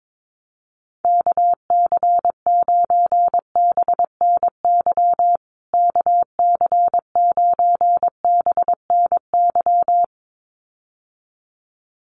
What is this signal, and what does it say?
Morse code, heard as a keyed tone, about 22 wpm, 700 Hz. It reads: KC9BNY KC9BNY
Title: USA Call Signs Repeated 2x 22wpm